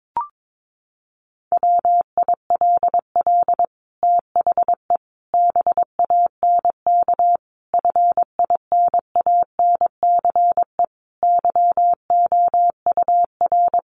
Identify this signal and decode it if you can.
Morse code, heard as a keyed tone, about 22 wpm, 700 Hz. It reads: WILL THE BANK FINANCE YOUR
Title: Sentences from Top 1500 Words 22wpm